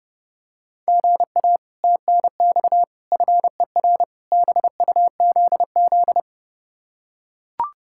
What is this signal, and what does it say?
Morse code, heard as a keyed tone, about 30 wpm, 700 Hz. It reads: GA TNX FER BUZZ